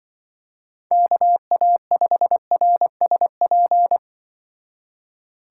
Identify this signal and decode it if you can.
Morse code, heard as a keyed tone, about 23 wpm, 700 Hz. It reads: KA5RSP
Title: Speed Racing - USA Call Signs 20wpm